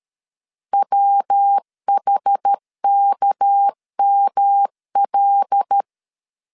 Morse code, heard as a keyed tone, about 13 wpm, 800 Hz. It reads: WHKML
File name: LEZIONI / 6 PKX0 / 6 PK 12wpm 2006